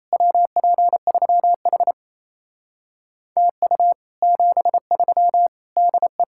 Morse code, heard as a keyed tone, about 37 wpm, 700 Hz. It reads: WP3H TU 73 DE